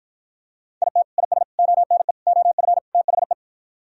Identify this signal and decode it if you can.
Morse code, heard as a keyed tone, about 51 wpm, 700 Hz. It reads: IT IS ONE OF THE